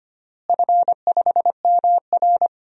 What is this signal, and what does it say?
Morse code, heard as a keyed tone, about 25 wpm, 700 Hz. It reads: F5MR
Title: DXCC-100 Call Signs - 4 Characters - Repeated 3x - 25wpm